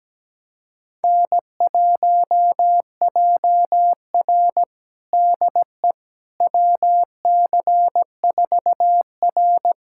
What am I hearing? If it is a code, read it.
Morse code, heard as a keyed tone, about 17 wpm, 700 Hz. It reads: N1JR DE WC4R
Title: USA Callsign DE Callsign - 4 Characters - 17wpm